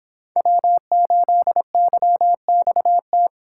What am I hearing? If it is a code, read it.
Morse code, heard as a keyed tone, about 26 wpm, 700 Hz. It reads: W8YXT